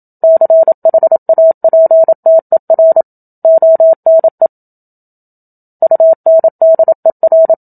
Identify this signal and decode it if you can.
Morse code, heard as a keyed tone, about 27 wpm, 650 Hz. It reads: CHAPTER ONE UNDER
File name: Book Two / 01-Chapter 1 - 27 WPM